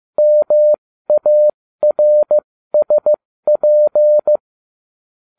Morse code, heard as a keyed tone, about 15 wpm, spaced as for 13 wpm, 600 Hz. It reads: MARSP